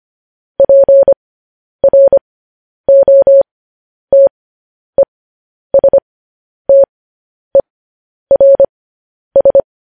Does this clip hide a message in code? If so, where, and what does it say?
Yes: Morse code from 0.6 to 9.6 s, heard as a keyed tone, about 25 wpm, spaced as for 10 wpm, 550 Hz. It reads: PROTESTERS